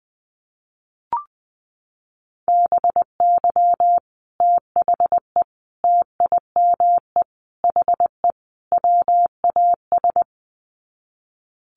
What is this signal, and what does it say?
Morse code, heard as a keyed tone, about 20 wpm, 700 Hz. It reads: BY THE TIME HE WAS
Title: Sets of 5 Words Encore 20wpm